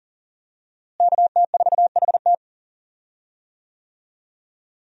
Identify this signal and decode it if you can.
Morse code, heard as a keyed tone, about 40 wpm, 700 Hz. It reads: KT4HT
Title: USA Call Signs - 5 Characters - Repeated 2x - 40wpm